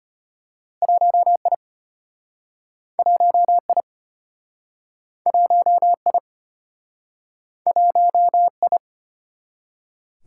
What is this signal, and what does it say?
Morse code, heard as a keyed tone, about 34 wpm, 700 Hz. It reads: 1I 1I 1I 1I